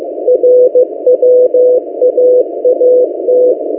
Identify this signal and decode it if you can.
Morse code, heard as a keyed tone, about 15 wpm, 500 Hz. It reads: RWAAT
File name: M08a- -1355utc-8096khz